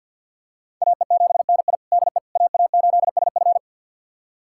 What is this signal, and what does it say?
Morse code, heard as a keyed tone, about 52 wpm, 700 Hz. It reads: AE7NI DE AA8SF
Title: USA Callsign DE Callsign - 5 Characters - 50wpm